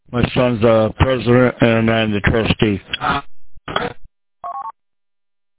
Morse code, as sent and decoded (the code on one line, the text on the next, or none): none